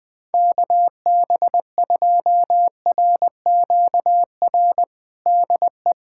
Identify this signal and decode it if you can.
Morse code, heard as a keyed tone, about 20 wpm, 700 Hz. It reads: KB2RQR DE
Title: USA Callsign DE Callsign - 6 Characters - 20wpm